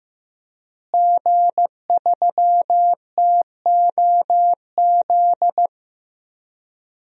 Morse code, heard as a keyed tone, about 15 wpm, 700 Hz. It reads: G3TOZ
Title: DXCC-100 Call Signs - 5 Characters - Repeated 3x - 15wpm